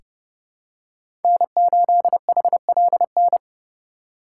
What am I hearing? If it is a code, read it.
Morse code, heard as a keyed tone, about 30 wpm, 700 Hz. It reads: N8HLN
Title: USA Call Signs - 5 Characters - 6x Character Spacing - 30wpm